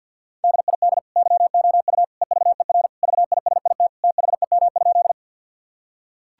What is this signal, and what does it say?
Morse code, heard as a keyed tone, about 50 wpm, 700 Hz. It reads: DID YOU EVER VISIT THEM?
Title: Sentences from Top 1000 Words - Review Part 6 - 50wpm